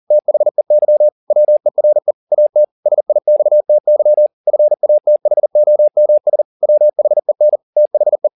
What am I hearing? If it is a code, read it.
Morse code, heard as a keyed tone, about 40 wpm, 600 Hz. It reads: THEY WERE AT SIXTY FATHOMS WHEN THE